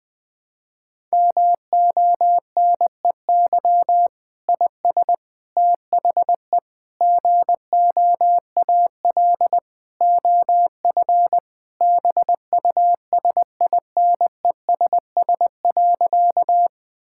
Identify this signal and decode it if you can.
Morse code, heard as a keyed tone, about 20 wpm, 700 Hz. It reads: MONEY IS THE GOAL OF BUSINESS.
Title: Sentences from Top 1000 Words - Review Part 5 - 20wpm